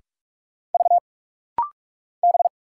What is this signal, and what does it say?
Morse code, heard as a keyed tone, about 44 wpm, 700 Hz. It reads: V B
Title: Single Letter-Number Alt - Rapid-Fire 45wpm